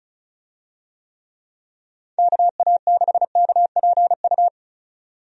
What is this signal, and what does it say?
Morse code, heard as a keyed tone, about 36 wpm, 700 Hz. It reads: KA6KPU
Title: USA Call Signs - 6 Characters - 35wpm